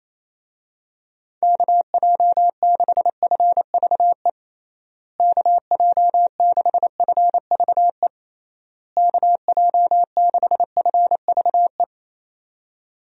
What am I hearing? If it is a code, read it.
Morse code, heard as a keyed tone, about 28 wpm, 700 Hz. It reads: KJ6FVE KJ6FVE KJ6FVE
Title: USA Call Signs - 6 Characters - Repeated 3x - 28wpm